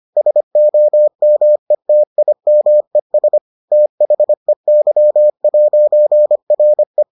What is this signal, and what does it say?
Morse code, heard as a keyed tone, about 25 wpm, 600 Hz. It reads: SOMETIMES THEY'RE